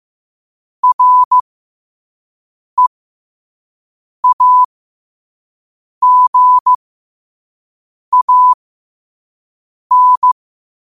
Morse code, heard as a keyed tone, about 15 wpm, spaced as for 5 wpm, 1000 Hz. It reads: REAGAN
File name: qotdfiles / 5wpm